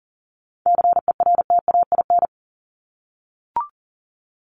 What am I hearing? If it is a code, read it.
Morse code, heard as a keyed tone, about 40 wpm, 700 Hz. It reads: CERTAIN
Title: LY Suffix - Top 100 Words with Base Words 40wpm